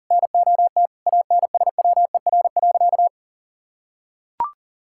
Morse code, heard as a keyed tone, about 40 wpm, 700 Hz. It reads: NOT ANSWER.
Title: Sentences from Top 1000 Words - Review Part 6 - 40wpm